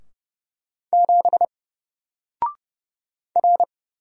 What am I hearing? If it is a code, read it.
Morse code, heard as a keyed tone, about 30 wpm, 700 Hz. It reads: 7 R
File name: Single-Letter-Number-30wpm